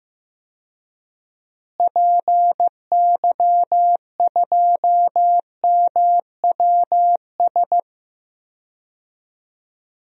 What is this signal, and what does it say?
Morse code, heard as a keyed tone, about 15 wpm, 700 Hz. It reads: PY2MWS
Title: DXCC-100 Call Signs - 6 Characters - Repeated 2x - 15wpm